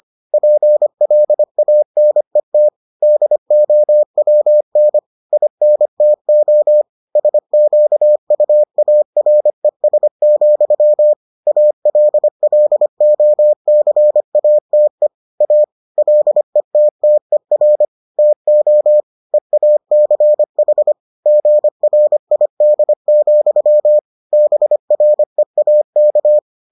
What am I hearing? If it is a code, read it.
Morse code, heard as a keyed tone, about 25 wpm, 600 Hz. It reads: PLANET DOWN INTO SQUARES, ALLOCATE A LETTER TO EACH GRID, BREAK